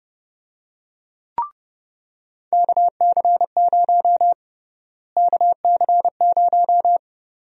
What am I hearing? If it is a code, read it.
Morse code, heard as a keyed tone, about 30 wpm, 700 Hz. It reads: KC0 KC0
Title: USA Call Sign Prefixes - Repeated 2x - 30wpm